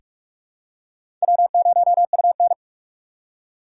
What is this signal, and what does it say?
Morse code, heard as a keyed tone, about 45 wpm, 700 Hz. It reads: W0UN
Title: USA Call Signs - 4 Characters - 3x Character Spacing - 45wpm